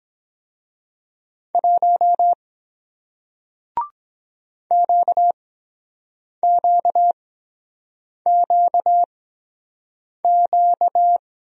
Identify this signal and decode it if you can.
Morse code, heard as a keyed tone, about 23 wpm, spaced as for 7 wpm, 700 Hz. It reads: 1 QQQQ